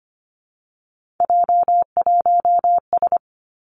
Morse code, heard as a keyed tone, about 25 wpm, 700 Hz. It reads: J1S
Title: Three Challenging Letter-Number Combinations 25wpm